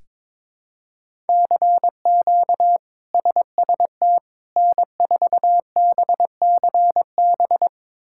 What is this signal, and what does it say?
Morse code, heard as a keyed tone, about 22 wpm, 700 Hz. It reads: CQ SST N4BCB
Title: K1USN SST QSO Practice - 22wpm